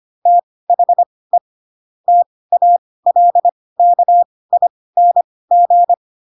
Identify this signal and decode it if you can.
Morse code, heard as a keyed tone, about 25 wpm, spaced as for 18 wpm, 700 Hz. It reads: THE TALKING
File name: Pinocchio / Pinocchio ch4 12 25 ik1hge